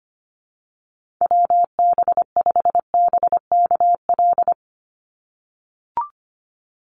Morse code, heard as a keyed tone, about 25 wpm, 700 Hz. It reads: WB5BKL
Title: Call Signs - 25wpm